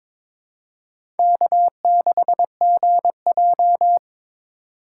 Morse code, heard as a keyed tone, about 22 wpm, 700 Hz. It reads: K6GJ
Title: USA Call Signs - 4 Characters - 2x Character Spacing - 22wpm